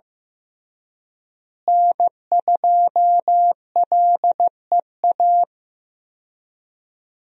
Morse code, heard as a keyed tone, about 15 wpm, 700 Hz. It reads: N2LEA
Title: USA Call Signs - 5 Characters - Repeated 2x - 15wpm